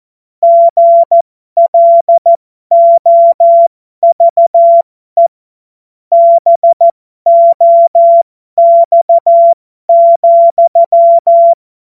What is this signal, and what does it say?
Morse code, heard as a keyed tone, about 14 wpm, spaced as for 12 wpm, 680 Hz. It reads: GLOVE BOX,